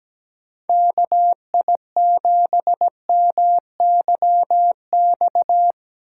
Morse code, heard as a keyed tone, about 17 wpm, 700 Hz. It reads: KI7MYX
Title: USA Call Signs - 6 Characters - 4x Character Spacing - 17wpm